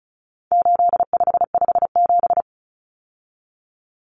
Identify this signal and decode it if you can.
Morse code, heard as a keyed tone, about 34 wpm, 700 Hz. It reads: 8557
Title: Numbers - 4-Digit 35wpm